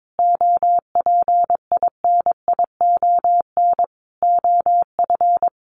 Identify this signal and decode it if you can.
Morse code, heard as a keyed tone, about 22 wpm, 700 Hz. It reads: OPINION OF
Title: Sentences from Top 1300 Words 22wpm